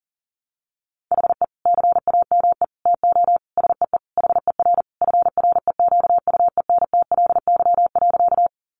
Morse code, heard as a keyed tone, about 40 wpm, 700 Hz. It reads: HE CAME TO SEE HER FREQUENTLY.
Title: Sentences from Top 1300 Words 40wpm